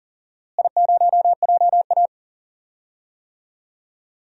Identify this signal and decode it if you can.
Morse code, heard as a keyed tone, about 40 wpm, 700 Hz. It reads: I0JA
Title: DXCC-100 Call Signs - 4 Characters - Repeated 3x - 40wpm